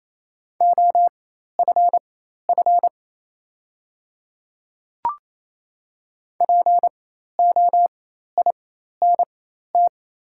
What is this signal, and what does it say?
Morse code, heard as a keyed tone, about 27 wpm, spaced as for 13 wpm, 690 Hz. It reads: OFF POINT
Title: 4x Character Spacing - Top 200 Words - 28wpm